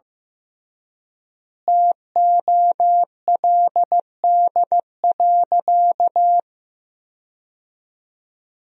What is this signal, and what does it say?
Morse code, heard as a keyed tone, about 15 wpm, 700 Hz. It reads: TOLD.
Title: 3x Word-Spacing - Sentences from Top 400 Words - 15wpm